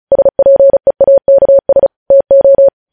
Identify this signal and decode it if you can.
Morse code, heard as a keyed tone, about 36 wpm, 550 Hz. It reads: SPEAKS TO
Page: News headlines in Morse code at 35 wpm